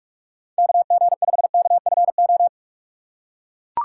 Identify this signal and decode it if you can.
Morse code, heard as a keyed tone, about 48 wpm, 700 Hz. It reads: KG5KFO
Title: USA Call Signs - 6 Characters - 6x Character Spacing - 45wpm